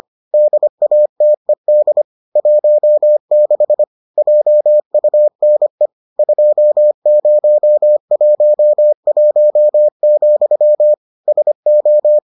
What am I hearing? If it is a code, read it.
Morse code, heard as a keyed tone, about 25 wpm, 600 Hz. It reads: DATED 16 JUNE 2011, SO